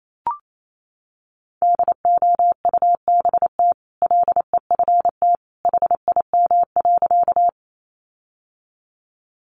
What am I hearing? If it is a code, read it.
Morse code, heard as a keyed tone, about 28 wpm, 700 Hz. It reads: DOUBT LEFT HIM.
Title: Sentences from Top 1500 Words 28wpm